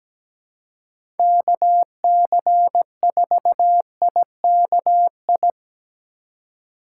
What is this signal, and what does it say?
Morse code, heard as a keyed tone, about 17 wpm, 700 Hz. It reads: KC4IKI
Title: USA Call Signs - 6 Characters - 2x Character Spacing - 17wpm